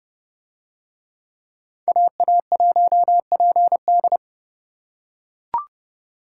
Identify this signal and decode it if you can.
Morse code, heard as a keyed tone, about 28 wpm, 700 Hz. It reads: AA1PD